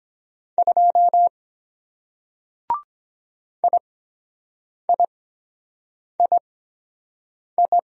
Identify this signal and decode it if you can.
Morse code, heard as a keyed tone, about 23 wpm, spaced as for 7 wpm, 700 Hz. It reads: 2 IIII